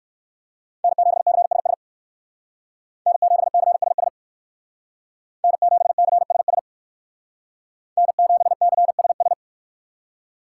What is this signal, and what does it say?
Morse code, heard as a keyed tone, about 63 wpm, 700 Hz. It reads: N7CSS N7CSS N7CSS N7CSS